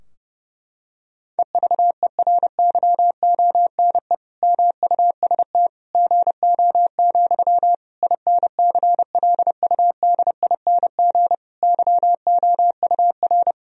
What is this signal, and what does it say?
Morse code, heard as a keyed tone, about 32 wpm, 700 Hz. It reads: EVERYONE MUST GO, INCLUDING YOUR
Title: Sentences from Top 1000 Words - Review Part 4 - 30wpm